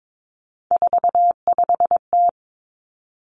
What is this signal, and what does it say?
Morse code, heard as a keyed tone, about 22 wpm, 700 Hz. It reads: 45T
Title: Three Challenging Letter-Number Combinations 22wpm